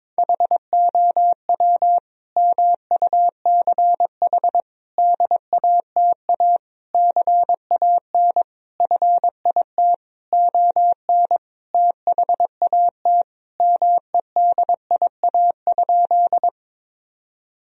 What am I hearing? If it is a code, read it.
Morse code, heard as a keyed tone, about 22 wpm, 700 Hz. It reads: HOW MUCH DATA CAN FIT ON THAT MEDIA?
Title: Sentences from Top 1000 Words - Review Part 4 - 22wpm